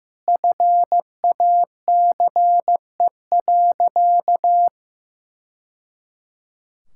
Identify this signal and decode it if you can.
Morse code, heard as a keyed tone, about 16 wpm, 700 Hz. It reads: FACE.
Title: Sentences from Top 1000 Words - Review Part 5 - 15wpm